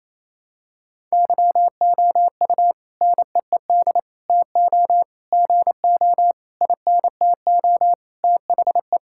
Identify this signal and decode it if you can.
Morse code, heard as a keyed tone, about 28 wpm, 700 Hz. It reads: YOU NEED TO GO INTO THE
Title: Sentences from Top 1000 Words - Review Part 6 - 28wpm